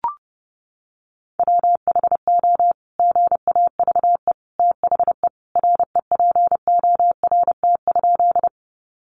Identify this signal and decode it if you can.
Morse code, heard as a keyed tone, about 30 wpm, 700 Hz. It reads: WHO GAVE THE REPORT?
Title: Sentences from Top 900 Words 30wpm